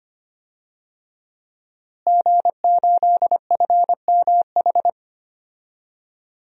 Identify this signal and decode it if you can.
Morse code, heard as a keyed tone, about 25 wpm, 700 Hz. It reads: G8FMH
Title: DXCC-100 Call Signs - 5 Characters - Repeated 3x - 25wpm